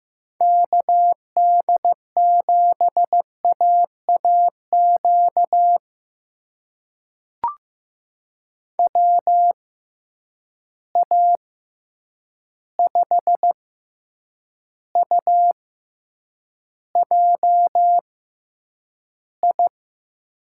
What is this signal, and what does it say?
Morse code, heard as a keyed tone, about 15 wpm, spaced as for 8 wpm, 700 Hz. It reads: KD7AAQ WA5UJI
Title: USA Call Signs - 6 Characters - 6x Character Spacing - 15wpm